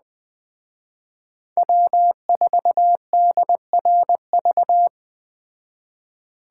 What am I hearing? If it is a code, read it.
Morse code, heard as a keyed tone, about 20 wpm, 700 Hz. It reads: W4DRV